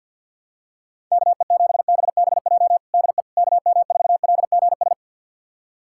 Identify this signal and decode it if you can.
Morse code, heard as a keyed tone, about 50 wpm, 700 Hz. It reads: KE7BBJ DE KM4LGS
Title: USA Callsign DE Callsign 50wpm